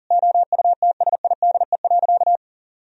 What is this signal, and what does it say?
Morse code, heard as a keyed tone, about 40 wpm, 700 Hz. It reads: OUTSIDE.
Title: Sentences from Top 1000 Words - Review Part 4 - 40wpm